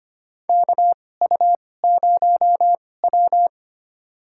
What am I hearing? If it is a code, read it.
Morse code, heard as a keyed tone, about 25 wpm, spaced as for 18 wpm, 700 Hz. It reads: KU0W
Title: USA Call Signs - 4 Characters - 2x Character Spacing - 25wpm